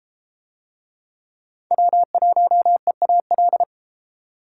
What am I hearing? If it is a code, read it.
Morse code, heard as a keyed tone, about 30 wpm, 700 Hz. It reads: W1EAL